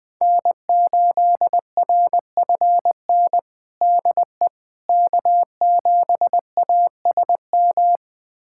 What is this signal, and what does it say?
Morse code, heard as a keyed tone, about 21 wpm, 700 Hz. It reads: N8RFN DE K7ASM
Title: USA Callsign DE Callsign - 5 Characters - 20wpm